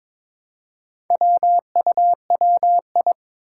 Morse code, heard as a keyed tone, about 22 wpm, 700 Hz. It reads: WUWI
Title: Four Challenging Letter-Number Combinations 22wpm